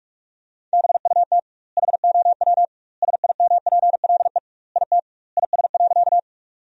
Morse code, heard as a keyed tone, about 46 wpm, 700 Hz. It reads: BUT HOW SIMPLE IT IS.
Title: 2x Word-Spacing - Sentences from Top 400 Words - 45wpm